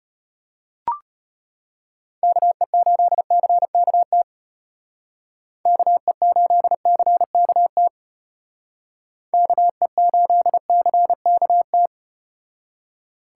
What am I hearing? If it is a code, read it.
Morse code, heard as a keyed tone, about 34 wpm, 700 Hz. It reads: KE8CKT KE8CKT KE8CKT